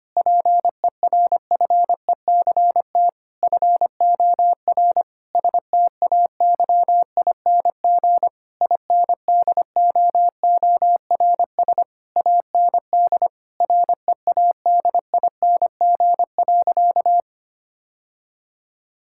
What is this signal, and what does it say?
Morse code, heard as a keyed tone, about 25 wpm, 700 Hz. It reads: PERFECT FOR STAYING INDOORS AND READING.
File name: practice / 25wpm / Difficult Sentences with Hints - 25wpm